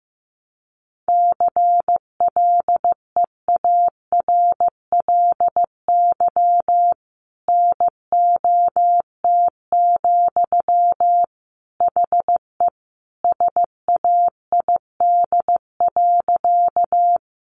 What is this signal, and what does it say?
Morse code, heard as a keyed tone, about 15 wpm, 700 Hz. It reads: CLEARLY NOT, HE SAID.
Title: Sentences from Top 1300 Words 15wpm